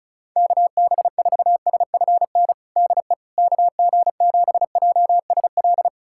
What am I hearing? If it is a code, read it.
Morse code, heard as a keyed tone, about 36 wpm, 700 Hz. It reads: KB4SFN DE KG7JSL
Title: USA Callsign DE Callsign - 6 Characters - 35wpm